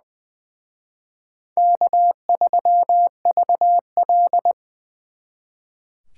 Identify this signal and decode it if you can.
Morse code, heard as a keyed tone, about 24 wpm, 700 Hz. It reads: K3VL
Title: Speed Racing - USA 1x2 Call Signs 20wpm